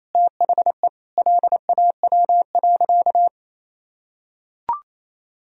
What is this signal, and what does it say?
Morse code, heard as a keyed tone, about 28 wpm, 700 Hz. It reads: THE LAW.
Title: Sentences from Top 1000 Words - Review Part 3 - 28wpm